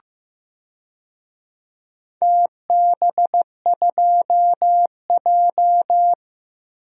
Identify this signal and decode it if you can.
Morse code, heard as a keyed tone, about 15 wpm, 700 Hz. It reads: TB2J
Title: DXCC-100 Call Signs - 4 Characters - 15wpm